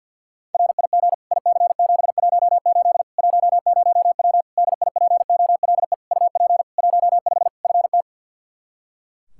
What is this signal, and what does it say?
Morse code, heard as a keyed tone, about 51 wpm, 700 Hz. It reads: RIG IC718 10W DIPOLE UP 15 FT
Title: Practice QSO - Exchange 2 - Info - 50wpm